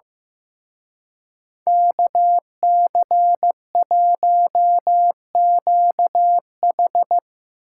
Morse code, heard as a keyed tone, about 16 wpm, 700 Hz. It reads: KC1QH
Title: Speed Racing - USA Call Signs 15wpm